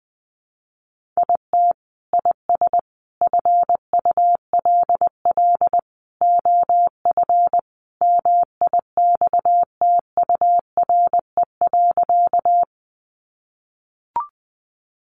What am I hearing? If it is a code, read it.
Morse code, heard as a keyed tone, about 20 wpm, 700 Hz. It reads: IT IS FULL OF MIXTURE.
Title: Sentences from Top 1500 Words 20wpm